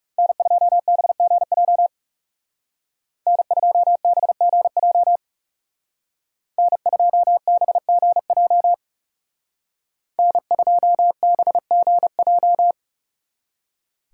Morse code, heard as a keyed tone, about 35 wpm, 700 Hz. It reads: N2BGJ N2BGJ N2BGJ N2BGJ
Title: Speed Racing - USA 1x3 Call Signs 30wpm